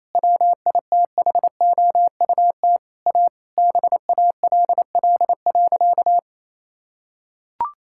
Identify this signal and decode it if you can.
Morse code, heard as a keyed tone, about 28 wpm, 700 Hz. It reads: WITHOUT A BALL.
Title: Sentences from Top 1000 Words - Review Part 6 - 28wpm